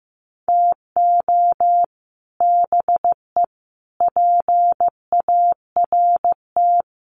Morse code, heard as a keyed tone, about 15 wpm, 700 Hz. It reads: TO BE PART
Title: Sets of 4 Words 15wpm